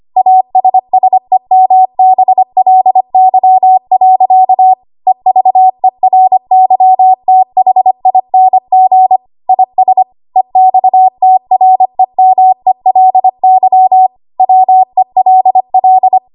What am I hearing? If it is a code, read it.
Morse code, heard as a keyed tone, about 25 wpm, 750 Hz. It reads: ASSEMBLY. EVERYTHING IS EXTREMELY WELL
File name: Morse / 25WPM